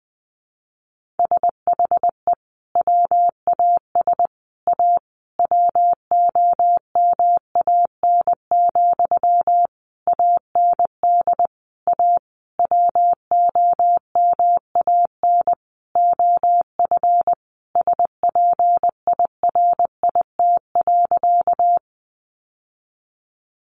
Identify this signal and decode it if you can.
Morse code, heard as a keyed tone, about 20 wpm, 700 Hz. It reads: SHE WAS A WOMAN, AND A WOMAN OF SPIRIT.
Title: Sentences from Top 1400 Words 20wpm